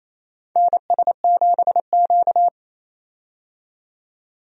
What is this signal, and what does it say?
Morse code, heard as a keyed tone, about 28 wpm, 700 Hz. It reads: NS7Q
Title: USA Call Signs - 4 Characters - 28wpm